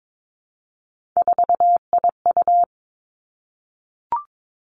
Morse code, heard as a keyed tone, about 22 wpm, 700 Hz. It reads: 4IU
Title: Three Challenging Letter-Number Combinations 22wpm